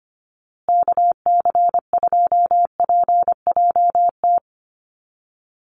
Morse code, heard as a keyed tone, about 25 wpm, 700 Hz. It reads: KC2PJT
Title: USA Call Signs Repeated 2x 25wpm